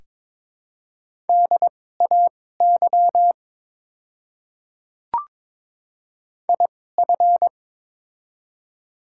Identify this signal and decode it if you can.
Morse code, heard as a keyed tone, about 22 wpm, spaced as for 16 wpm, 700 Hz. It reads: DAY IF
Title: 2x Character Spacing - Top 100 Words - 22wpm